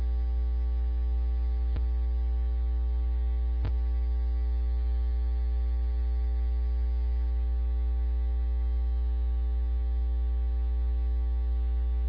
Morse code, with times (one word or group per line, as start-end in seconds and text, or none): none